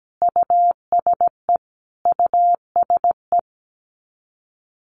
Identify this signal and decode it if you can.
Morse code, heard as a keyed tone, about 17 wpm, 700 Hz. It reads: USE USE
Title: Top 200 Words Repeated 17wpm